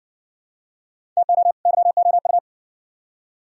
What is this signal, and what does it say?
Morse code, heard as a keyed tone, about 60 wpm, 700 Hz. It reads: TO YOU